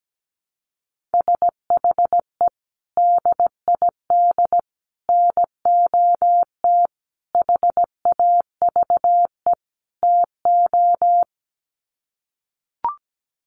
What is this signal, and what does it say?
Morse code, heard as a keyed tone, about 17 wpm, 700 Hz. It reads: SHE DID NOT HAVE TO
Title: Sets of 5 Words Encore2 17wpm